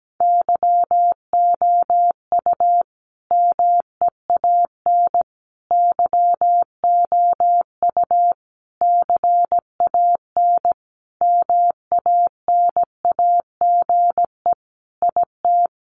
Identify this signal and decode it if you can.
Morse code, heard as a keyed tone, about 17 wpm, 700 Hz. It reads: YOU MEAN YOU CAN MANAGE IT
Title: Sentences from Top 1400 Words 17wpm